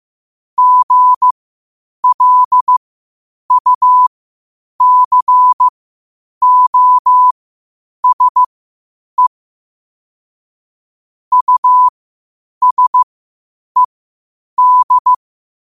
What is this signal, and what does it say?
Morse code, heard as a keyed tone, about 15 wpm, spaced as for 8 wpm, 1000 Hz. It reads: GLUCOSE USED
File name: qotdfiles / 8wpm